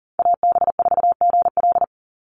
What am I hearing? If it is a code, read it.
Morse code, heard as a keyed tone, about 40 wpm, 700 Hz. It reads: AB4GL
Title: USA Call Signs 40wpm